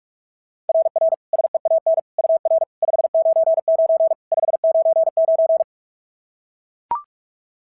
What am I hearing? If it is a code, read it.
Morse code, heard as a keyed tone, about 44 wpm, 690 Hz. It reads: RR SEAN UR 599 599
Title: SOTA QSO Practice 45wpm